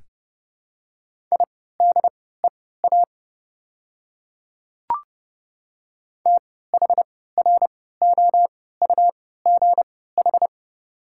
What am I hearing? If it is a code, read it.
Morse code, heard as a keyed tone, about 30 wpm, spaced as for 17 wpm, 700 Hz. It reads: IDEA THROUGH
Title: 3x Character Spacing - Top 300 Words - 30wpm